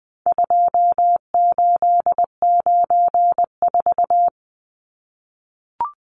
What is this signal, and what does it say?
Morse code, heard as a keyed tone, about 20 wpm, 700 Hz. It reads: 2894
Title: Numbers - 4-Digit 20wpm